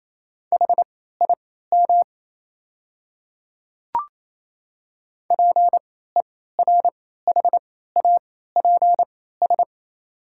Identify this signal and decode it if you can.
Morse code, heard as a keyed tone, about 28 wpm, spaced as for 16 wpm, 690 Hz. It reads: HIM PERHAPS
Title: 3x Character Spacing - Top 500 Words - 28wpm